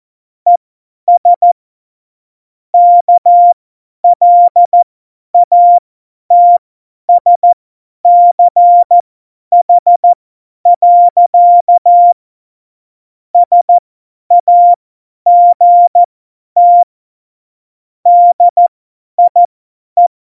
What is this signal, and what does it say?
Morse code, heard as a keyed tone, about 14 wpm, spaced as for 10 wpm, 700 Hz. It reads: ES KLATSCH. SAGT DIE